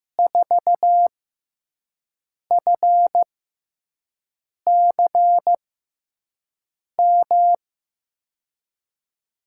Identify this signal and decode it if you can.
Morse code, heard as a keyed tone, about 15 wpm, spaced as for 5 wpm, 700 Hz. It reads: 4FCM
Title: USA Call Signs - 6 Characters - 6x Character Spacing - 15wpm